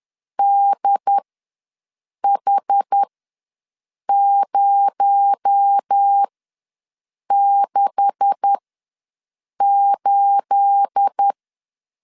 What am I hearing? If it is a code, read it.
Morse code, heard as a keyed tone, about 10 wpm, spaced as for 6 wpm, 800 Hz. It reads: DH068
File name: LEZIONI / 10 SIGNS / 10B INT ws 10wpm 4134